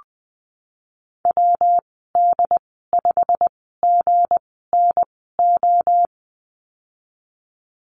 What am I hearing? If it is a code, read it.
Morse code, heard as a keyed tone, about 20 wpm, spaced as for 15 wpm, 700 Hz. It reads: WD5GNO
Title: 179 Lesson 29g - Slash - Call Signs 20wpm